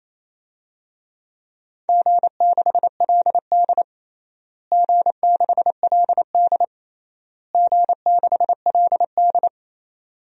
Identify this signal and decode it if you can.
Morse code, heard as a keyed tone, about 28 wpm, 700 Hz. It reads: G6LD G6LD G6LD
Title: DXCC-100 Call Signs - 4 Characters - Repeated 3x - 28wpm